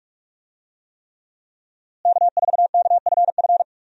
Morse code, heard as a keyed tone, about 46 wpm, 700 Hz. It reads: K4KFF
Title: USA Call Signs - 5 Characters - Repeated 3x - 45wpm